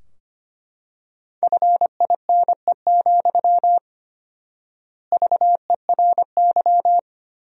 Morse code, heard as a keyed tone, about 25 wpm, 700 Hz. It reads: FINE, VERY
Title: 4x Word-Spacing - Sentences from Top 500 Words - 25wpm